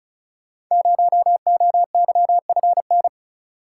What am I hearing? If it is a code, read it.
Morse code, heard as a keyed tone, about 34 wpm, 700 Hz. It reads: 0OYFN
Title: Code Groups - 5 Letter:Numbers - 35wpm